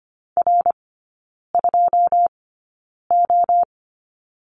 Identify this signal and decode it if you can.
Morse code, heard as a keyed tone, about 25 wpm, spaced as for 9 wpm, 690 Hz. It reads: R2O
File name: Lesson 15e - Number 5 - Three Characters 25wpm